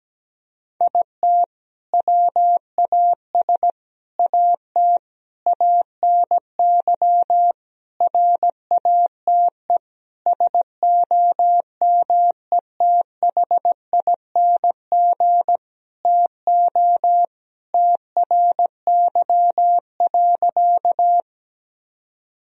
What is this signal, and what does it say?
Morse code, heard as a keyed tone, about 17 wpm, 700 Hz. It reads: IT WAS AT ANY RATE SOMETHING TO TRY.
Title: Sentences from Top 1000 Words - Review Part 4 - 17wpm